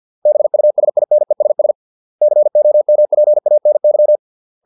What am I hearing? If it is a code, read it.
Morse code, heard as a keyed tone, about 50 wpm, 600 Hz. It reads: BUSINESS COMPANY
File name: 632 words / 632 Words 50WPM